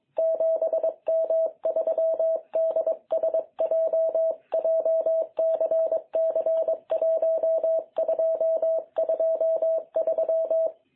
Morse code, heard as a keyed tone, about 23 wpm, 660 Hz. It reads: 7M3DSJ JCC1223
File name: jh8cbh / 2021cw